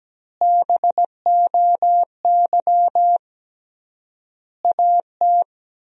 Morse code, heard as a keyed tone, about 17 wpm, 700 Hz. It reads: BOY AT
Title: 3x Word-Spacing - Sentences from Top 300 Words - 17wpm